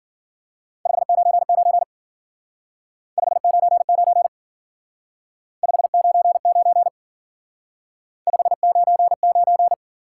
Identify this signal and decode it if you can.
Morse code, heard as a keyed tone, about 51 wpm, 700 Hz. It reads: 599 599 599 599